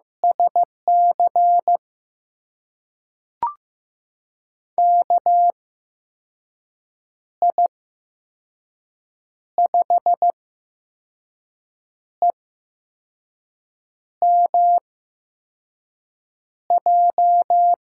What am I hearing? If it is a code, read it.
Morse code, heard as a keyed tone, about 15 wpm, spaced as for 5 wpm, 700 Hz. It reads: SC KI5EMJ
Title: USA Call Signs - 6 Characters - 8x Character Spacing - 15wpm